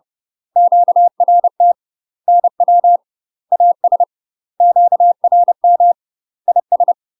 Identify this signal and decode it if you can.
Morse code, heard as a keyed tone, about 30 wpm, 700 Hz. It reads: QRT NW AS QRM IS
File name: UK MASTER QSOs TRAINING / 30WPM / UK MASTER QSOs 0048